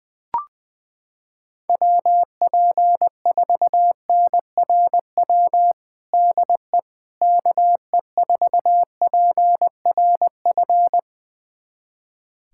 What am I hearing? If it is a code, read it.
Morse code, heard as a keyed tone, about 20 wpm, 700 Hz. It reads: WP4NRW DE KE4PRF
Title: USA Callsign DE Callsign - 6 Characters - 20wpm